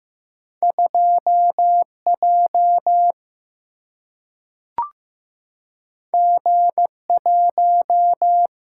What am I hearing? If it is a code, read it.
Morse code, heard as a keyed tone, about 15 wpm, 700 Hz. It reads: 2J G1
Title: Single-Letter Single-Number 15wpm